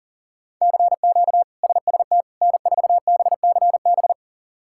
Morse code, heard as a keyed tone, about 40 wpm, 700 Hz. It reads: CQ SST N4BCB
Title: K1USN SST QSO Practice - 40wpm